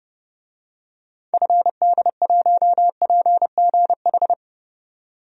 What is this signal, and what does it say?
Morse code, heard as a keyed tone, about 30 wpm, 700 Hz. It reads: FD1PGH
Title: DXCC-100 Call Signs - 6 Characters - 30wpm